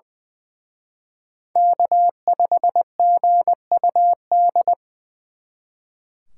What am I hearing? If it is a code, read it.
Morse code, heard as a keyed tone, about 25 wpm, 700 Hz. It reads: K5GUD
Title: Speed Racing - USA 1x3 Call Signs 20wpm